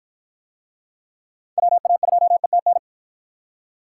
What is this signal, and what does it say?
Morse code, heard as a keyed tone, about 52 wpm, 700 Hz. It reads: WA2ETN